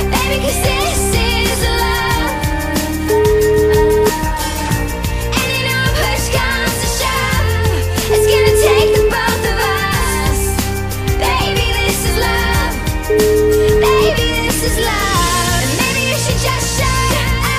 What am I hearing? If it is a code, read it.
Ringback tone (Europe, 425 Hz), with noise.